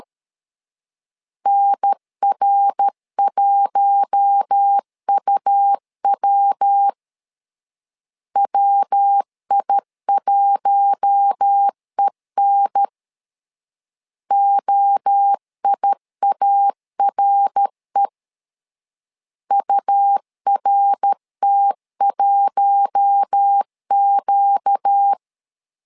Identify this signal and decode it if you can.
Morse code, heard as a keyed tone, about 13 wpm, 800 Hz. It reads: NR1UW WI1EN OIARE URT1Q
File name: LEZIONI / 4 Q1 12wpm 2028